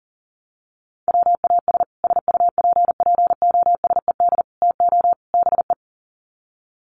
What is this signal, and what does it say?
Morse code, heard as a keyed tone, about 40 wpm, 700 Hz. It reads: WAS SUPPOSED TO BE
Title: Sets of 4 Words 40wpm